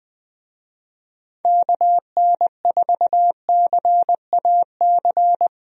Morse code, heard as a keyed tone, about 20 wpm, 700 Hz. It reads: KN4CAC
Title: USA Call Signs - 6 Characters - Repeated 2x - 20wpm